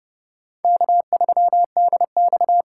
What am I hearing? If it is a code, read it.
Morse code, heard as a keyed tone, about 28 wpm, 700 Hz. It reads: K3DX